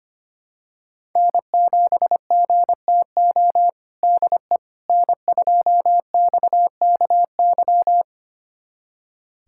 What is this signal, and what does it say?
Morse code, heard as a keyed tone, about 25 wpm, 700 Hz. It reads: N7GTO DE N2XKY
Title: USA Callsign DE Callsign - 5 Characters - 25wpm